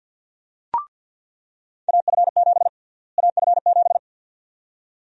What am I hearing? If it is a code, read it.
Morse code, heard as a keyed tone, about 46 wpm, 700 Hz. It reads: AF7 AF7
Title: USA Call Sign Prefixes - Repeated 2x - 50wpm